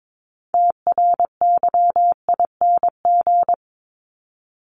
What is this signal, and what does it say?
Morse code, heard as a keyed tone, about 22 wpm, 700 Hz. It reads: TRYING
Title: ING Suffix - Top 100 Words with Base Words 22wpm